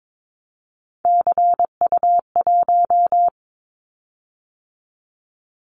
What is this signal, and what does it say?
Morse code, heard as a keyed tone, about 22 wpm, 700 Hz. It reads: CU1
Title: Callsign Prefixes for Top 100 DXCC - 22wpm